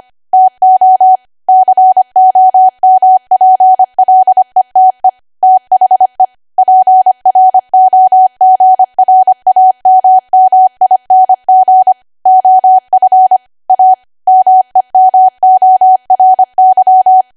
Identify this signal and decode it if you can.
Morse code, heard as a keyed tone, about 25 wpm, 750 Hz. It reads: TO COMPLETE THE PROGRAMMING OF A MEMORY
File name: Archive / 25 WPM / 25WPM